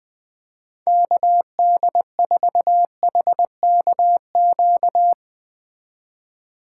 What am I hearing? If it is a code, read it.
Morse code, heard as a keyed tone, about 20 wpm, 700 Hz. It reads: KD4HKQ
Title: Speed Racing - USA 2x3 Call Signs 20wpm